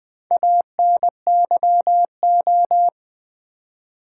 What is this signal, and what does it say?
Morse code, heard as a keyed tone, about 20 wpm, 700 Hz. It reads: ANYO
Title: Build A Word - 2 Letters At A Time - 6-Letter Words - 20wpm